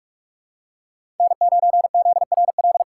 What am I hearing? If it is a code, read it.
Morse code, heard as a keyed tone, about 44 wpm, 700 Hz. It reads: N9ZRL
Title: USA Call Signs - 6x Character Spacing - 45wpm